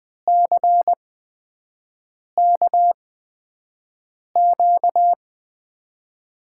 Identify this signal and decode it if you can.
Morse code, heard as a keyed tone, about 20 wpm, spaced as for 5 wpm, 700 Hz. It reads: CKQ
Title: USA Call Signs - 6 Characters - 8x Character Spacing - 20wpm